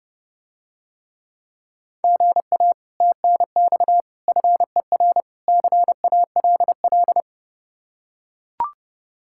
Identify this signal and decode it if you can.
Morse code, heard as a keyed tone, about 31 wpm, 700 Hz. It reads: GA TNX FER CALL